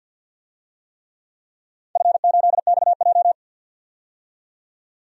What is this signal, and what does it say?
Morse code, heard as a keyed tone, about 47 wpm, 700 Hz. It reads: F8XJ
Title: DXCC-100 Call Signs - 4 Characters - 50wpm